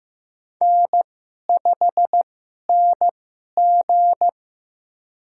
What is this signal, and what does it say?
Morse code, heard as a keyed tone, about 15 wpm, spaced as for 11 wpm, 700 Hz. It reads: N5NG
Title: USA Call Signs - 4 Characters - 2x Character Spacing - 15wpm